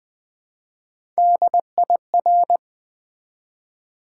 Morse code, heard as a keyed tone, about 20 wpm, 700 Hz. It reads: DIR
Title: Build A Word - 1 Letter At A Time - 3 to 5-Letter Words - 20wpm